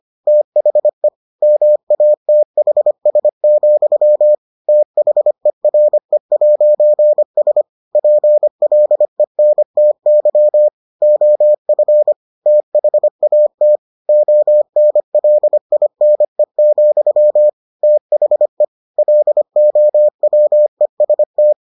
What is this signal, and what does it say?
Morse code, heard as a keyed tone, about 25 wpm, 600 Hz. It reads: THE MATHS, THERE'S PLENTY OF THAT ONLINE, THE LOWEST